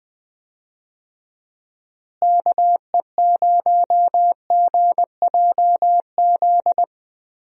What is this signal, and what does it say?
Morse code, heard as a keyed tone, about 20 wpm, 700 Hz. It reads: KE0GJZ
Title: USA Call Signs - 6 Characters - Repeated 2x - 20wpm